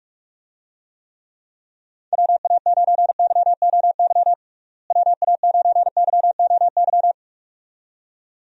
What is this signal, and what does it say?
Morse code, heard as a keyed tone, about 44 wpm, 700 Hz. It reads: WA9YOY WA9YOY
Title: USA Call Signs - 6 Characters - Repeated 2x - 45wpm